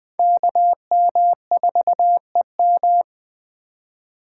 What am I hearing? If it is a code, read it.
Morse code, heard as a keyed tone, about 20 wpm, 700 Hz. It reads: KM4EM
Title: USA Call Signs - 5 Characters - 6x Character Spacing - 20wpm